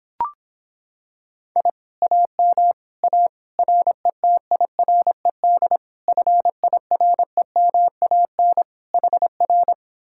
Morse code, heard as a keyed tone, about 26 wpm, 700 Hz. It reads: I AM A RETIRED FIREMAN HR